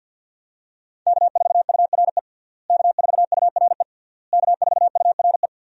Morse code, heard as a keyed tone, about 49 wpm, 700 Hz. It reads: K4URE K4URE K4URE
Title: USA Call Signs - 5 Characters - Repeated 3x - 50wpm